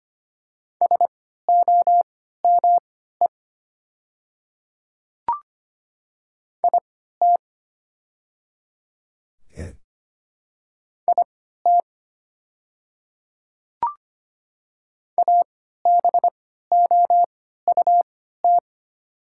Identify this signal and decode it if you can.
Morse code, heard as a keyed tone, about 25 wpm, spaced as for 14 wpm, 700 Hz. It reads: SOME IT IT ABOUT
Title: 3x Character Spacing - Top 100 Words - 25wpm